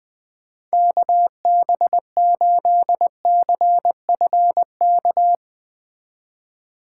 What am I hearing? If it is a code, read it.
Morse code, heard as a keyed tone, about 20 wpm, 700 Hz. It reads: KB8CFK
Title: Speed Racing - USA 2x3 Call Signs 17wpm